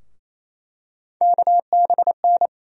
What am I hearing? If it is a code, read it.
Morse code, heard as a keyed tone, about 28 wpm, 700 Hz. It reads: KBN
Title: USA Call Sign Postfixes - 3 Letters 28wpm